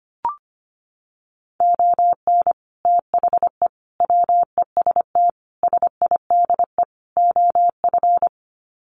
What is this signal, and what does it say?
Morse code, heard as a keyed tone, about 25 wpm, 700 Hz. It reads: ON THE WEST SIDE OF
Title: Sets of 5 Words Encore2 25wpm